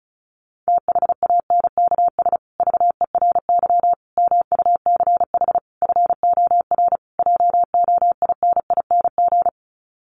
Morse code, heard as a keyed tone, about 34 wpm, 700 Hz. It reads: THANKS VERY MUCH FOR JOINING
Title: Sets of 5 Words Encore2 35wpm